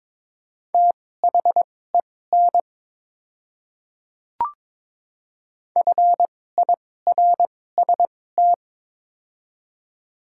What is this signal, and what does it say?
Morse code, heard as a keyed tone, about 22 wpm, spaced as for 16 wpm, 700 Hz. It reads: THEN FIRST
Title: 2x Character Spacing - Top 100 Words - 22wpm